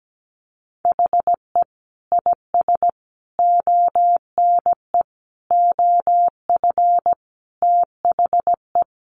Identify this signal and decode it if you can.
Morse code, heard as a keyed tone, about 17 wpm, 700 Hz. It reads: HE IS ONE OF THE
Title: Sets of 5 Words Encore 17wpm